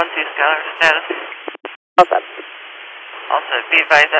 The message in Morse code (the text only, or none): none